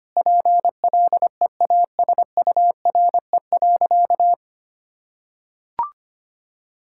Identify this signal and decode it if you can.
Morse code, heard as a keyed tone, about 25 wpm, 700 Hz. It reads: PLEASURE.
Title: Sentences from Top 1600 Words 25wpm